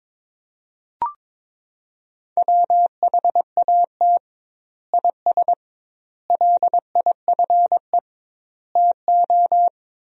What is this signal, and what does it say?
Morse code, heard as a keyed tone, about 22 wpm, 700 Hz. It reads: WHAT IS LIFE TO
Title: 2x Word-Spacing - Sentences from Top 300 Words - 22wpm